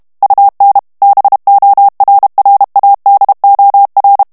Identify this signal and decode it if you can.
Morse code, heard as a keyed tone, about 32 wpm, 800 Hz. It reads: UN BORRADOR
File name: morse mp3 / ESPAGNOL TEXTE EN CLAIR 1 VITESSE 30 MOTS MINUTES